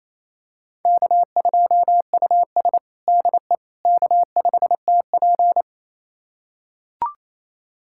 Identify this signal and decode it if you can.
Morse code, heard as a keyed tone, about 28 wpm, 700 Hz. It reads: K2US DE K5TP
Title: USA Callsign DE Callsign - 4 Characters - 28wpm